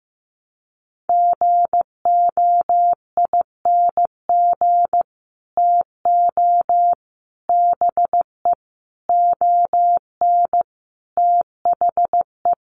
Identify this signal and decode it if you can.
Morse code, heard as a keyed tone, about 15 wpm, 700 Hz. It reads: GOING TO BE ON THE
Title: Sets of 5 Words Encore2 15wpm